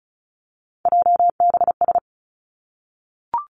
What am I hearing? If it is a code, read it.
Morse code, heard as a keyed tone, about 36 wpm, 700 Hz. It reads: JBS
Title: Three Challenging Letter-Number Combinations 35wpm